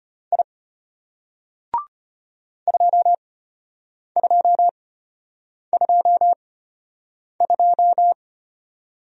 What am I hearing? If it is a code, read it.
Morse code, heard as a keyed tone, about 34 wpm, spaced as for 8 wpm, 700 Hz. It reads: I 2222